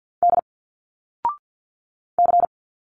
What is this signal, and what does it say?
Morse code, heard as a keyed tone, about 50 wpm, 700 Hz. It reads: D C
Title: Single Letters 50wpm